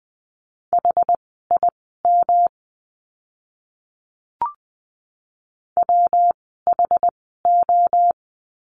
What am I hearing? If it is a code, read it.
Morse code, heard as a keyed tone, about 20 wpm, spaced as for 14 wpm, 700 Hz. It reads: HIM WHO
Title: 146 Lesson 24f - Letter y - Words 20wpm